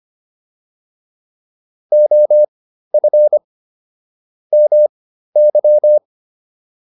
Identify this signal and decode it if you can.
Morse code, heard as a keyed tone, about 25 wpm, spaced as for 13 wpm, 600 Hz. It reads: OF MY